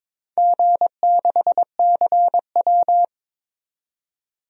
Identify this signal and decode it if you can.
Morse code, heard as a keyed tone, about 22 wpm, 700 Hz. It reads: G6CW
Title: DXCC-100 Call Signs - 4 Characters - 22wpm